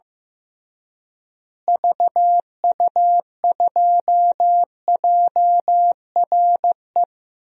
Morse code, heard as a keyed tone, about 16 wpm, 700 Hz. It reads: VU2JRE
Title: DXCC-100 Call Signs - 6 Characters - Repeated 3x - 15wpm